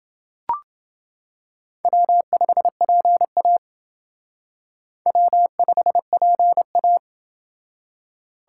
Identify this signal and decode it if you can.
Morse code, heard as a keyed tone, about 28 wpm, 700 Hz. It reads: W5PA W5PA